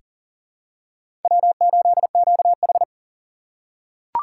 Morse code, heard as a keyed tone, about 40 wpm, 700 Hz. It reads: W8QH
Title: USA Call Signs - 4 Characters - 4x Character Spacing - 40wpm